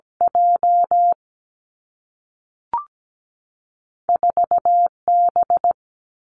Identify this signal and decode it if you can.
Morse code, heard as a keyed tone, about 17 wpm, 700 Hz. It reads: J 4 B
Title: Two Challenging Letter-Number Combinations 17wpm